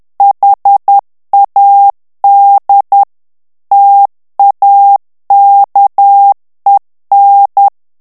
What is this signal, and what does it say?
Morse code, heard as a keyed tone, about 10 wpm, 800 Hz. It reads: HAD TAKEN
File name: morse mp3 / ANGLAIS texte clair 2 vitesse 10 mots minute